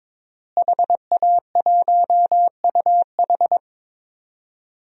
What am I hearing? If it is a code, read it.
Morse code, heard as a keyed tone, about 21 wpm, 700 Hz. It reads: HA1UH
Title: DXCC-100 Call Signs - 5 Characters - Repeated 3x - 22wpm